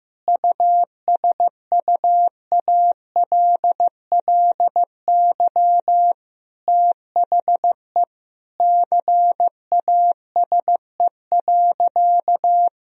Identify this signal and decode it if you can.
Morse code, heard as a keyed tone, about 16 wpm, 700 Hz. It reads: USUALLY THE CASE.
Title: Sentences from Top 1000 Words - Review Part 3 - 15wpm